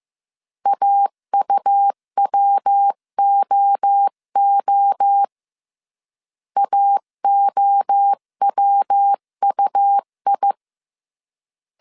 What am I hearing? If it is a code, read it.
Morse code, heard as a keyed tone, about 15 wpm, 800 Hz. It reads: AUWOO AOWUI
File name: CW / DET / 1 3 1500C S A 14wpm 2447